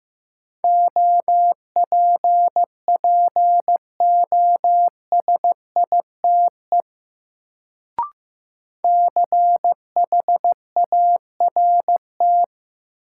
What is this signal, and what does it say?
Morse code, heard as a keyed tone, about 15 wpm, 700 Hz. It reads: OPPOSITE CHART
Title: Sentences from Top 1300 Words - New Word Drill - 15wpm